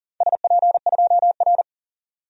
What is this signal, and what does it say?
Morse code, heard as a keyed tone, about 42 wpm, 700 Hz. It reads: SP2R
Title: DXCC-100 Call Signs - 4 Characters - 40wpm